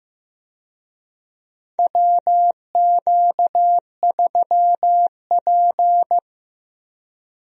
Morse code, heard as a keyed tone, about 15 wpm, 700 Hz. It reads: WQ3P
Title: USA Call Signs - 4 Characters - Repeated 3x - 15wpm